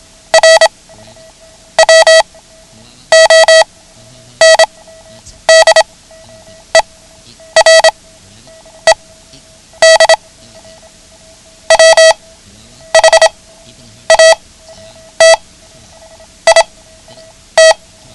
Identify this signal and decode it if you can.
Morse code, heard as a keyed tone, about 26 wpm, spaced as for 9 wpm, 650 Hz. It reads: R WONDERED WHAT IT
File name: ham radio / CodeQuick / Code Quick 09